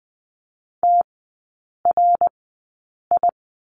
Morse code, heard as a keyed tone, about 20 wpm, spaced as for 8 wpm, 700 Hz. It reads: TRI
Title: 053 Lesson 10e - Letter r - Three Characters 20wpm